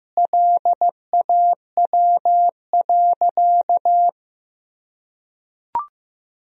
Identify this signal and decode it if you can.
Morse code, heard as a keyed tone, about 15 wpm, 700 Hz. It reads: LAW.
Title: Sentences from Top 1000 Words - Review Part 3 - 15wpm